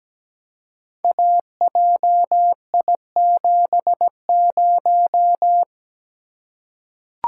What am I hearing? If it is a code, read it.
Morse code, heard as a keyed tone, about 17 wpm, 700 Hz. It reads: AJI70
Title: Code Groups - 5 Letter:Numbers - 17wpm